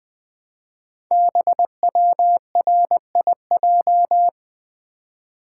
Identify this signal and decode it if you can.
Morse code, heard as a keyed tone, about 20 wpm, 700 Hz. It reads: BWRIJ
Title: Code Groups - 5 Letters - 20wpm